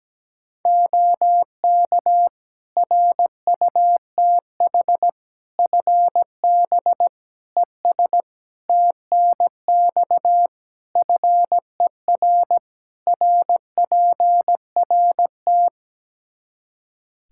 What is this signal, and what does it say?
Morse code, heard as a keyed tone, about 17 wpm, 700 Hz. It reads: OK RUTH FB ES TNX FER RPRT
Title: Practice QSO - Exchange 2 - Info - 17wpm